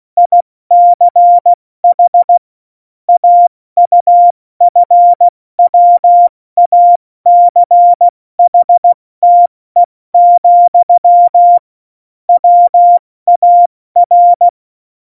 Morse code, heard as a keyed tone, about 16 wpm, 700 Hz. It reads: ICH AUFWACHTE, WAR
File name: Morsekurs / Witze / Witz 26. Dezember 2017, 70 F80